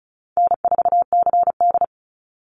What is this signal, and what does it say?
Morse code, heard as a keyed tone, about 34 wpm, 700 Hz. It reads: N4CD
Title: Easy Call Signs Repeated 2x - 35wpm